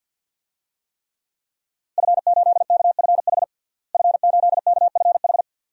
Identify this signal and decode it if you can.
Morse code, heard as a keyed tone, about 46 wpm, 700 Hz. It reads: F8KFH F8KFH
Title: DXCC-100 Call Signs - 5 Characters - Repeated 2x - 50wpm